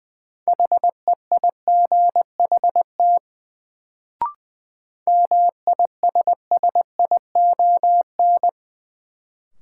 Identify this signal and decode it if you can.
Morse code, heard as a keyed tone, about 20 wpm, 700 Hz. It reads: HEIGHT MISSION
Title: Sentences from Top 1600 Words - New Word Drill - 20wpm